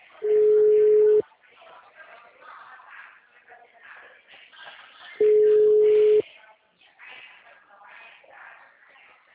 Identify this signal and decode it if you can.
Ringback tone (Europe, 425 Hz), with speech.
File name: out- -2014- - - .18